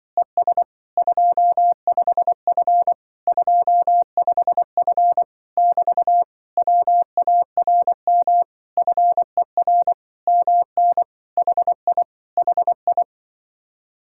Morse code, heard as a keyed tone, about 24 wpm, 700 Hz. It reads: ES 25F 25F <BT> WARM FER MN HI HI
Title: Speed-Racing Common QSO Phrases Encore - 20wpm